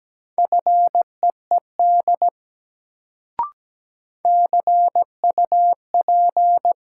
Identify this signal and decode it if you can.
Morse code, heard as a keyed tone, about 17 wpm, 700 Hz. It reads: FEED CUP
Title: Sentences from Top 1600 Words - New Word Drill Alt - 17wpm